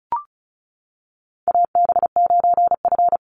Morse code, heard as a keyed tone, about 34 wpm, 700 Hz. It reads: AB9F
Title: USA Call Signs 35wpm